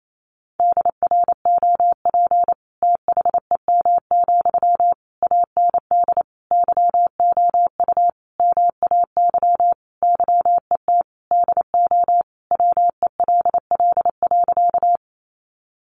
Morse code, heard as a keyed tone, about 28 wpm, 700 Hz. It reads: DROP THEM, AND YOU MAY YET DO WELL.
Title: Sentences from Top 1400 Words 28wpm